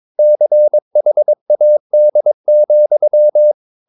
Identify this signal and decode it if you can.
Morse code, heard as a keyed tone, about 22 wpm, 600 Hz. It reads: CHAD,
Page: Romanen Never af Ken Follet på engelsk.